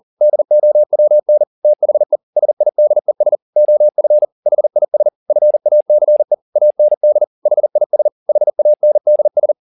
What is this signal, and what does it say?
Morse code, heard as a keyed tone, about 40 wpm, 600 Hz. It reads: DOWN THE SIDES OF HIS FACE AND HIS HANDS
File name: the old man and the sea 1 40wpm